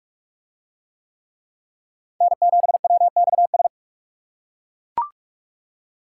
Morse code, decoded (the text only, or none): N7WXS